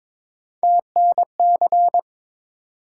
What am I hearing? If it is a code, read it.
Morse code, heard as a keyed tone, about 22 wpm, 700 Hz. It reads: TNC
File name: practice / 22wpm / Three Letter-Number Combinations - 22wpm